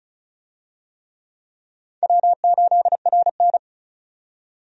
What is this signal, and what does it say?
Morse code, heard as a keyed tone, about 36 wpm, 700 Hz. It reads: W8RN
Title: USA Call Signs - 4 Characters - 35wpm